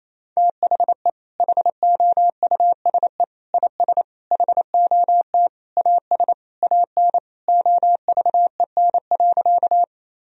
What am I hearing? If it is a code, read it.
Morse code, heard as a keyed tone, about 28 wpm, 700 Hz. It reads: THE HOUSE IS HOT AS AN OVEN.
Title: Sentences from Top 1000 Words - Review Part 4 - 28wpm